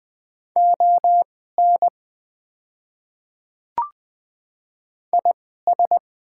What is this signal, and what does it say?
Morse code, heard as a keyed tone, about 20 wpm, spaced as for 15 wpm, 700 Hz. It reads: ON IS
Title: 2x Character Spacing - Top 100 Words - 20wpm